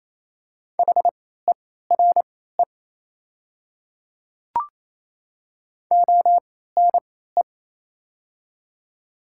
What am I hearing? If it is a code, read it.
Morse code, heard as a keyed tone, about 28 wpm, spaced as for 16 wpm, 700 Hz. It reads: HERE ONE
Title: 3x Character Spacing - Top 300 Words - 28wpm